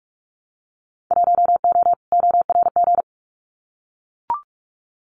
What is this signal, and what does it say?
Morse code, heard as a keyed tone, about 44 wpm, 700 Hz. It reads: 1O ORG
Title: Winter Field Day Exchanges 45wpm